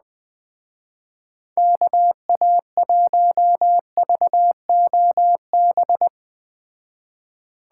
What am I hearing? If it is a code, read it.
Morse code, heard as a keyed tone, about 20 wpm, 700 Hz. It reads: KA1VOB
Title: Speed Racing - USA 2x3 Call Signs 17wpm